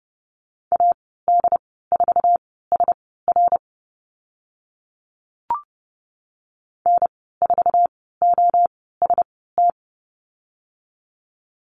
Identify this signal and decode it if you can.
Morse code, heard as a keyed tone, about 30 wpm, spaced as for 17 wpm, 700 Hz. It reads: AD4SR N4OST
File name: Lesson 15g - Number 5 - Call Signs 30wpm